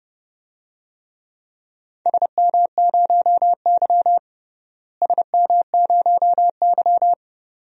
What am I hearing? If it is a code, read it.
Morse code, heard as a keyed tone, about 32 wpm, 700 Hz. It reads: SM0Y SM0Y
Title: DXCC-100 Call Signs - 4 Characters - Repeated 2x - 30wpm